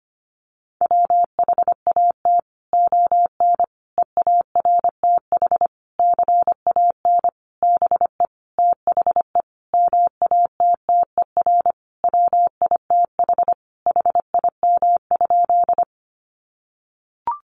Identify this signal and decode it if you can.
Morse code, heard as a keyed tone, about 25 wpm, 700 Hz. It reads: WHAT ON EARTH CAN BE THE MATTER WITH HIM?
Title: Sentences from Top 700 Words 25wpm